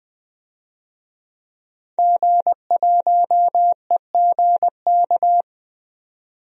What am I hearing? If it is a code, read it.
Morse code, heard as a keyed tone, about 20 wpm, 700 Hz. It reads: G1EGK
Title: DXCC-100 Call Signs - 5 Characters - 20wpm